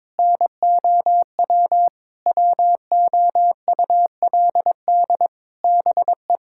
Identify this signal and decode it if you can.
Morse code, heard as a keyed tone, about 22 wpm, 700 Hz. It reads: NOW WOULD BE
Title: Sentences from Top 1600 Words 22wpm